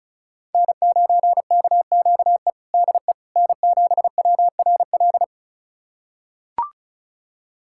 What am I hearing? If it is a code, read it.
Morse code, heard as a keyed tone, about 34 wpm, 690 Hz. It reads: N9KQE DE N7WRL
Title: USA Callsign DE Callsign - 5 Characters - 35wpm